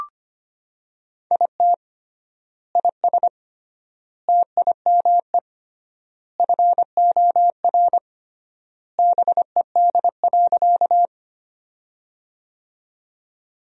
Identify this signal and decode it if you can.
Morse code, heard as a keyed tone, about 25 wpm, 700 Hz. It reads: IT IS TIME FOR BED.
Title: 3x Word-Spacing - Sentences from Top 500 Words - 25wpm